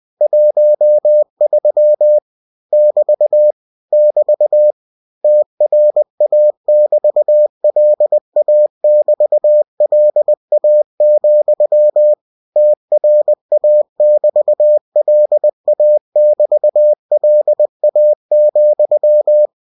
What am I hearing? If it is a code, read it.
Morse code, heard as a keyed tone, about 20 wpm, 600 Hz. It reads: 13 <BT> <BT> TRA<BT>LA<BT>LA, TRA<BT>LA<BT>LA,
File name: cw-books / winnie-the-pooh / winnie-20wpm-page-0013